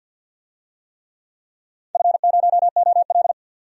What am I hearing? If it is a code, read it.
Morse code, heard as a keyed tone, about 47 wpm, 700 Hz. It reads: F0OL
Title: DXCC-100 Call Signs - 4 Characters - 50wpm